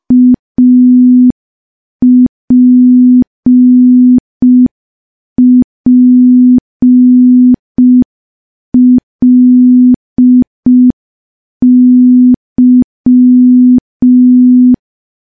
Morse code, heard as a keyed tone, about 5 wpm, 250 Hz. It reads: APPLY
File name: morse / morse-training-05WPM